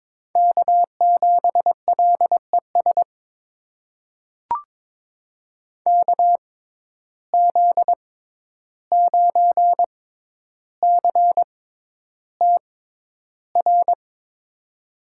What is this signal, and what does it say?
Morse code, heard as a keyed tone, about 22 wpm, spaced as for 11 wpm, 700 Hz. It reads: K7LES KZ9CTR
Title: USA Call Signs - 6x Character Spacing - 22wpm